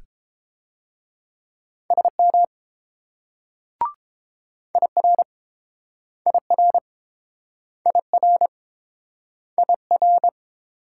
Morse code, heard as a keyed tone, about 29 wpm, 700 Hz. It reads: SM IR IR IR IR